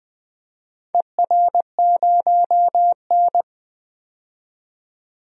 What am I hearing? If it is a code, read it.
Morse code, heard as a keyed tone, about 21 wpm, 700 Hz. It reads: ER0N
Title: DXCC-100 Call Signs - 4 Characters - 20wpm